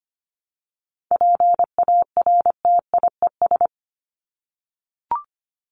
Morse code, heard as a keyed tone, about 25 wpm, 700 Hz. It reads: PARTIES
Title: ES:S Suffix - Top 100 Words with Base Words 25wpm